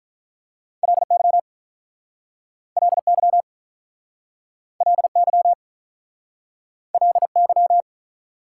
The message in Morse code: LY LY LY LY

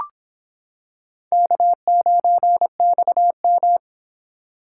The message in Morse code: K9XM

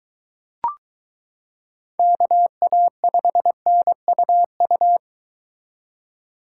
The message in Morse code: KA5NUU

